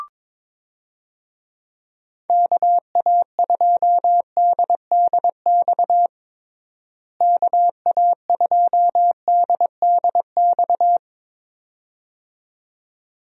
Morse code, heard as a keyed tone, about 22 wpm, 700 Hz. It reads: KA2DDX KA2DDX
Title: USA Call Signs - 6 Characters - Repeated 2x - 22wpm